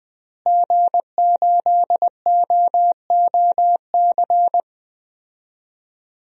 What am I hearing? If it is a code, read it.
Morse code, heard as a keyed tone, about 20 wpm, 700 Hz. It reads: G8OOC
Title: DXCC-100 Call Signs - 5 Characters - Repeated 3x - 20wpm